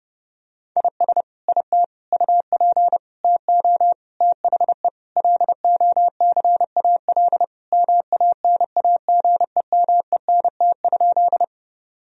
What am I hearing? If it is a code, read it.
Morse code, heard as a keyed tone, about 32 wpm, 700 Hz. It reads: IS IT UP TO THE LOCAL MANAGEMENT?
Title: Sentences from Top 1000 Words - Review Part 4 - 30wpm